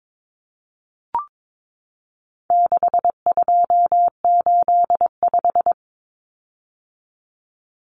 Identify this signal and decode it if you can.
Morse code, heard as a keyed tone, about 22 wpm, 700 Hz. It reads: 6285
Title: Common Cut Numbers - 4-Digit 22wpm